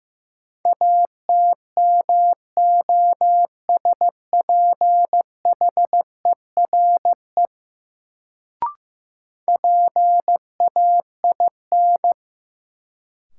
Morse code, heard as a keyed tone, about 15 wpm, 700 Hz. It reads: ATMOSPHERE PAIN
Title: Sentences from Top 1600 Words - New Word Drill - 15wpm